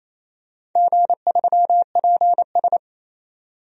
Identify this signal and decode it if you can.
Morse code, heard as a keyed tone, about 28 wpm, 700 Hz. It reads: G3PS
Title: DXCC-100 Call Signs - 4 Characters - 28wpm